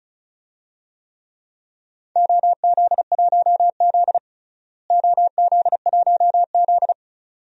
Morse code, heard as a keyed tone, about 36 wpm, 700 Hz. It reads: OZ1Z OZ1Z
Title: DXCC-100 Call Signs - 4 Characters - Repeated 2x - 35wpm